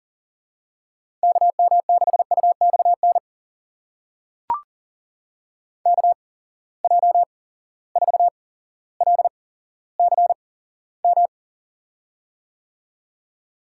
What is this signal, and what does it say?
Morse code, heard as a keyed tone, about 40 wpm, spaced as for 17 wpm, 700 Hz. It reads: KM6UXN KJ4LCM
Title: USA Call Signs - 6 Characters - 8x Character Spacing - 40wpm